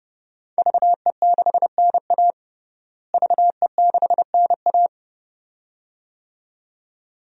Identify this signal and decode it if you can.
Morse code, heard as a keyed tone, about 30 wpm, 700 Hz. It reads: VE6NA VE6NA
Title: DXCC-100 Call Signs - 5 Characters - Repeated 2x - 30wpm